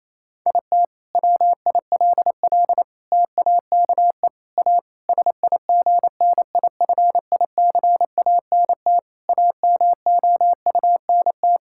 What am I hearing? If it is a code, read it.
Morse code, heard as a keyed tone, about 28 wpm, 700 Hz. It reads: IT WILL TAKE A SIGNIFICANT AMOUNT
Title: Sentences from Top 1000 Words - Review Part 6 - 28wpm